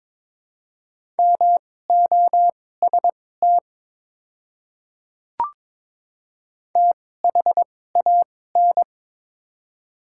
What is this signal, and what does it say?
Morse code, heard as a keyed tone, about 23 wpm, spaced as for 16 wpm, 700 Hz. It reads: MOST THAN
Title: 2x Character Spacing - Top 100 Words - 22wpm